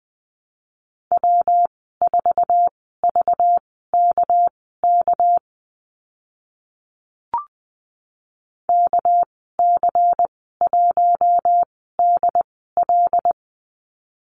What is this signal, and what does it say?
Morse code, heard as a keyed tone, about 20 wpm, spaced as for 15 wpm, 700 Hz. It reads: W4VKK KC1DL
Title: 207 Lesson 33g - Letter j - Call Signs 20wpm